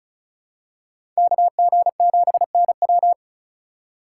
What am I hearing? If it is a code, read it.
Morse code, heard as a keyed tone, about 36 wpm, 700 Hz. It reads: KG7NW
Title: USA Call Signs - 5 Characters - 4x Character Spacing - 35wpm